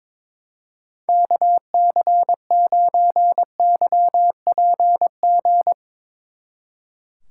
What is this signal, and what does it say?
Morse code, heard as a keyed tone, about 22 wpm, 700 Hz. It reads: KC9YPG
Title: Speed Racing - USA 2x3 Call Signs 22wpm